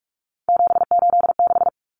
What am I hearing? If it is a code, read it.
Morse code, heard as a keyed tone, about 44 wpm, 700 Hz. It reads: 786
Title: Common Cut Numbers - 3-Digit 45wpm